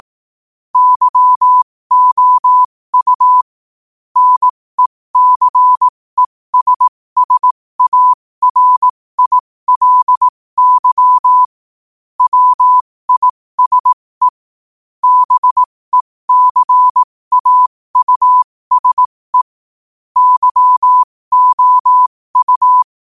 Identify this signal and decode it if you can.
Morse code, heard as a keyed tone, about 18 wpm, spaced as for 15 wpm, 1000 Hz. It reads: YOU NECESSARILY WISE BECAUSE YOU
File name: qotdfiles / 15wpm